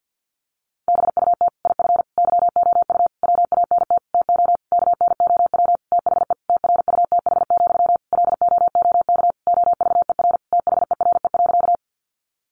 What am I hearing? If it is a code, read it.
Morse code, heard as a keyed tone, about 50 wpm, 700 Hz. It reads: BUT IF YOU WANT TO KNOW THE TRUTH, LOOK OVER THERE.
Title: Sentences from Top 800 Words 50wpm